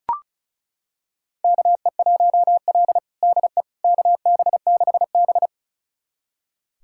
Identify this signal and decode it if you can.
Morse code, heard as a keyed tone, about 34 wpm, 700 Hz. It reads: KE1L DE KB6B
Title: USA Callsign DE Callsign - 4 Characters - 35wpm